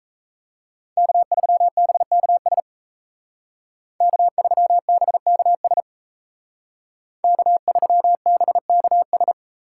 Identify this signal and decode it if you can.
Morse code, heard as a keyed tone, about 35 wpm, 700 Hz. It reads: K3BKS K3BKS K3BKS
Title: Speed Racing - USA 1x3 Call Signs 28wpm